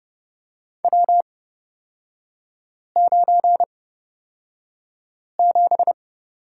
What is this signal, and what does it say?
Morse code, heard as a keyed tone, about 31 wpm, spaced as for 5 wpm, 700 Hz. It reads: W97
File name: Lesson 36e - BT - Three Characters 30wpm